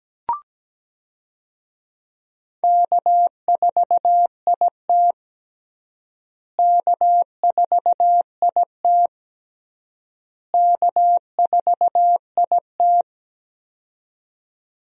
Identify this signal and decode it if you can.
Morse code, heard as a keyed tone, about 17 wpm, 700 Hz. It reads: K4IT K4IT K4IT
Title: USA Call Signs - 4 Characters - Repeated 3x - 17wpm